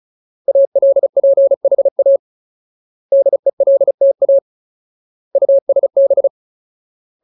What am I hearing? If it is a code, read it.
Morse code, heard as a keyed tone, about 34 wpm, 550 Hz. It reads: ALPHA DELTA USB
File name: Ultimate Word List / 35WPM Ultimate